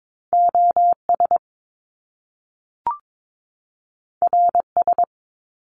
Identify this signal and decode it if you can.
Morse code, heard as a keyed tone, about 22 wpm, 700 Hz. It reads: OS RS
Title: Top Two Letter Combinations in Words 22wpm